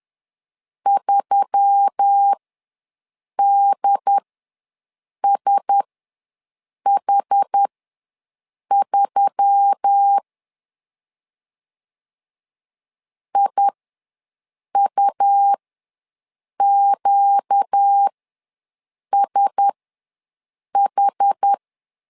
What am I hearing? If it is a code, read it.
Morse code, heard as a keyed tone, about 10 wpm, spaced as for 6 wpm, 800 Hz. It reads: 3DSH3 IUQSH
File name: LEZIONI / 4 Q1S3 ws 10wpm 2437